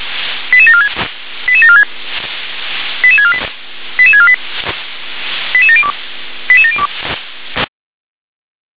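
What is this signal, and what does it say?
Five-tone selective call (ZVEI), heard as quick runs of five tones, tones of about 80 ms. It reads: 88746 88746 88648 88648 88702 88702
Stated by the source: ZVEI five-tone call received on an SDR.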